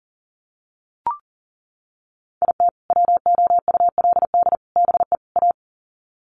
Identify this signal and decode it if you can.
Morse code, heard as a keyed tone, about 40 wpm, 700 Hz. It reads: IT WOULD BE A
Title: Sets of 4 Words 40wpm